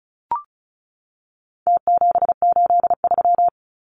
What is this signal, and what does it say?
Morse code, heard as a keyed tone, about 34 wpm, 700 Hz. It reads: T783
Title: Numbers - Contest Style - 4-Digit 35wpm